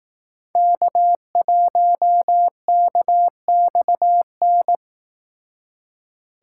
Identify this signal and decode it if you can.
Morse code, heard as a keyed tone, about 18 wpm, 700 Hz. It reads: K1KXN
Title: Speed Racing - USA 1x3 Call Signs 15wpm